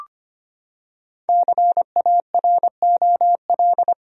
Morse code, heard as a keyed tone, about 25 wpm, 700 Hz. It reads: CAROL
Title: USA Common Names - 25wpm